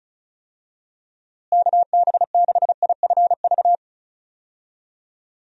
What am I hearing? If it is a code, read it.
Morse code, heard as a keyed tone, about 34 wpm, 700 Hz. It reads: KB6IFV
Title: USA Call Signs - 6 Characters - Repeated 3x - 35wpm